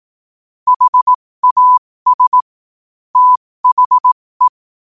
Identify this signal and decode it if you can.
Morse code, heard as a keyed tone, about 18 wpm, spaced as for 15 wpm, 1000 Hz. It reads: HAS THE
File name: qotdfiles / 15wpm